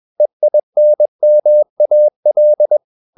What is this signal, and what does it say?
Morse code, heard as a keyed tone, about 21 wpm, 600 Hz. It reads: EINMAL
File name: marienkind2 21wpm